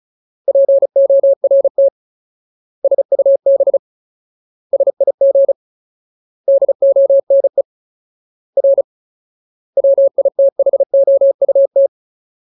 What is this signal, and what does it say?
Morse code, heard as a keyed tone, about 34 wpm, 550 Hz. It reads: PORT SUB SIG DONE R WITHOUT
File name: Ultimate Word List / 35WPM Ultimate